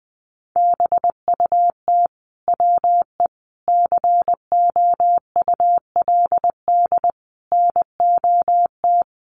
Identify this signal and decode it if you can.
Morse code, heard as a keyed tone, about 20 wpm, 700 Hz. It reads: BUT WE COULD NOT
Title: Sentences from Top 1100 Words 20wpm